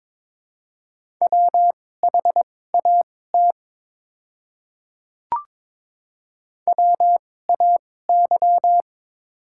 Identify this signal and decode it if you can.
Morse code, heard as a keyed tone, about 23 wpm, spaced as for 16 wpm, 700 Hz. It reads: WHAT WAY
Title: 2x Character Spacing - Top 200 Words - 22wpm